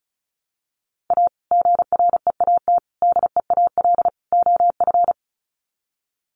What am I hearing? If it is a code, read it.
Morse code, heard as a keyed tone, about 34 wpm, 700 Hz. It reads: A GREAT DEAL OF
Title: Sets of 4 Words 35wpm